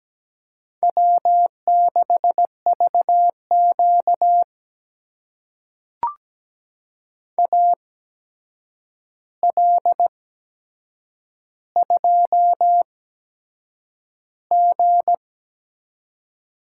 Morse code, heard as a keyed tone, about 17 wpm, spaced as for 7 wpm, 700 Hz. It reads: W6VQ AL2G
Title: USA Call Signs - 4 Characters - 8x Character Spacing - 17wpm